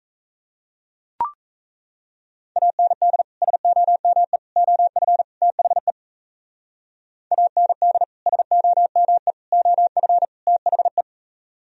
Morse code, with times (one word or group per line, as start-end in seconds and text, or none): 2.6-3.2 AND
3.4-4.4 SOME
4.6-5.2 OF
5.4-5.9 THE
7.3-8.0 AND
8.3-9.3 SOME
9.5-10.2 OF
10.5-11.0 THE